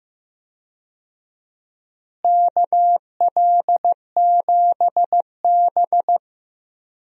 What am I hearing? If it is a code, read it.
Morse code, heard as a keyed tone, about 15 wpm, 700 Hz. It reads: KL7B
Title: DXCC-100 Call Signs - 4 Characters - Repeated 3x - 15wpm